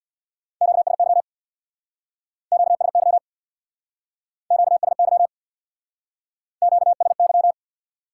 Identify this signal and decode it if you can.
Morse code, heard as a keyed tone, about 61 wpm, 700 Hz. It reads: QSY QSY QSY QSY
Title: Speed-Racing Top 180 QSO Elements 50wpm